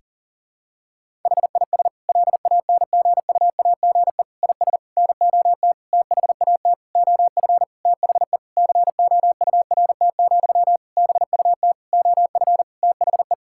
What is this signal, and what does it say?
Morse code, heard as a keyed tone, about 40 wpm, 700 Hz. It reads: HIS LANGUAGE IS NOT THAT OF THE COURT, BUT OF THE
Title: Sentences from Top 1600 Words 40wpm